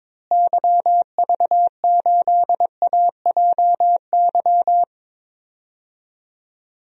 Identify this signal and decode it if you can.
Morse code, heard as a keyed tone, about 22 wpm, 700 Hz. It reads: YV8AJY
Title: DXCC-100 Call Signs - 6 Characters - Repeated 3x - 22wpm